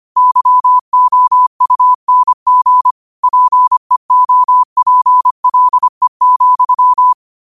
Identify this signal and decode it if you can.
Morse code, heard as a keyed tone, about 25 wpm, 1000 Hz. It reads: YOUNG PEOPLE,